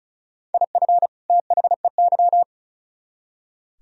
Morse code, heard as a keyed tone, about 45 wpm, 700 Hz. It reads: IF THEY